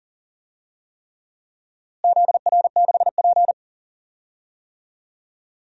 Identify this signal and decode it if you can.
Morse code, heard as a keyed tone, about 40 wpm, 700 Hz. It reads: ZR6P
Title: DXCC-100 Call Signs - 4 Characters - Repeated 3x - 40wpm